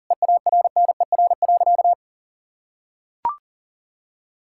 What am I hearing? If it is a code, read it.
Morse code, heard as a keyed tone, about 40 wpm, 700 Hz. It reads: EARNER.
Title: Pangram Challenge v2 - 40wpm